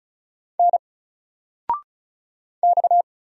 Morse code, heard as a keyed tone, about 36 wpm, 700 Hz. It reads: N X
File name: Single-Letter-Number-35wpm